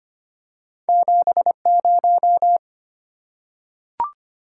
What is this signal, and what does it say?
Morse code, heard as a keyed tone, about 25 wpm, 700 Hz. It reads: 70
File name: Two Letter-Number Combinations - 25wpm